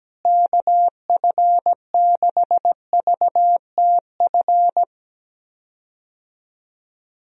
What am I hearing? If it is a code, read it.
Morse code, heard as a keyed tone, about 17 wpm, 700 Hz. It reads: KF6VTF
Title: USA Call Signs - 6 Characters - Repeated 3x - 17wpm